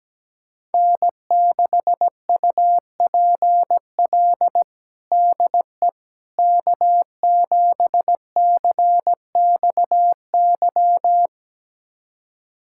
Morse code, heard as a keyed tone, about 17 wpm, 700 Hz. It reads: N6UPL DE K7CXY
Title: USA Callsign DE Callsign - 5 Characters - 17wpm